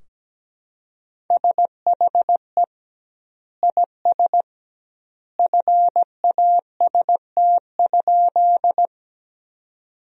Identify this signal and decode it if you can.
Morse code, heard as a keyed tone, about 17 wpm, 700 Hz. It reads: SHE IS FAST?
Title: 2x Word-Spacing - Sentences from Top 400 Words - 17wpm